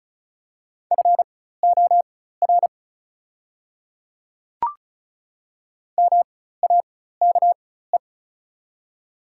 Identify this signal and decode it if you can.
Morse code, heard as a keyed tone, about 36 wpm, spaced as for 16 wpm, 700 Hz. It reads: FOR MAKE
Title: 4x Character Spacing - Top 100 Words - 35wpm